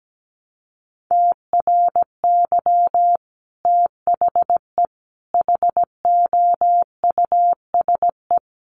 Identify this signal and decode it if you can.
Morse code, heard as a keyed tone, about 17 wpm, 700 Hz. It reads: TRY THE HOUSE
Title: Sentences from Top 1300 Words 17wpm